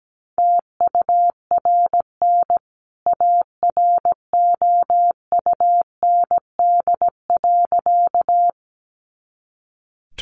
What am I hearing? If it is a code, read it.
Morse code, heard as a keyed tone, about 17 wpm, 700 Hz. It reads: TURN AROUND.
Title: A Sound Of Thunder 17wpm